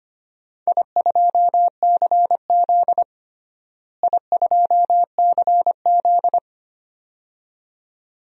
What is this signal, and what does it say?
Morse code, heard as a keyed tone, about 25 wpm, 700 Hz. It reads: I2CZ I2CZ
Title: DXCC-100 Call Signs - 4 Characters - Repeated 2x - 25wpm